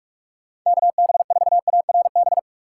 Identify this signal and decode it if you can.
Morse code, heard as a keyed tone, about 42 wpm, 700 Hz. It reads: KB4ARB
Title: USA Call Signs - 6 Characters - 4x Character Spacing - 45wpm